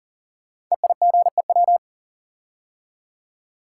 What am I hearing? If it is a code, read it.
Morse code, heard as a keyed tone, about 38 wpm, 690 Hz. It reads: EIGEW
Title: Code Groups - 5 Letters - 40wpm